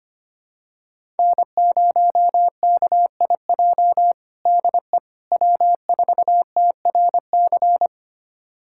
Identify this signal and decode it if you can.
Morse code, heard as a keyed tone, about 25 wpm, 700 Hz. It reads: N0KIJ DE W4TRC
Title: USA Callsign DE Callsign - 5 Characters - 25wpm